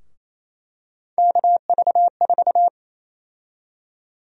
Morse code, heard as a keyed tone, about 28 wpm, 700 Hz. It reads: KV4
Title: USA Call Sign Prefixes - Repeated 2x - 28wpm